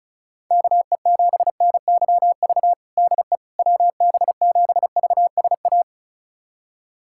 Morse code, heard as a keyed tone, about 36 wpm, 700 Hz. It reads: KE7NYV DE WB7VSA
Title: USA Callsign DE Callsign 35wpm